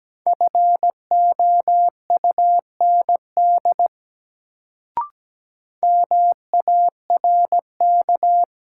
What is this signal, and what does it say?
Morse code, heard as a keyed tone, about 17 wpm, 700 Hz. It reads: FOUND MARK
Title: Sentences from Top 300 Words - New Word Drill - 17wpm